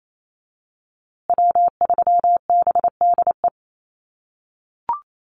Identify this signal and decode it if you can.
Morse code, heard as a keyed tone, about 28 wpm, 700 Hz. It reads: W3BDE
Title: USA Call Signs Repeated 2x 28wpm